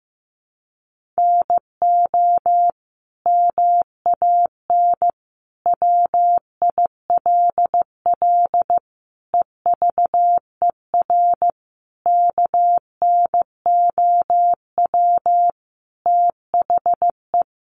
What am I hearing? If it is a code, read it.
Morse code, heard as a keyed tone, about 15 wpm, 700 Hz. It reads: NO MAN WILL EVER KNOW THE
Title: Sentences from Top 1300 Words 15wpm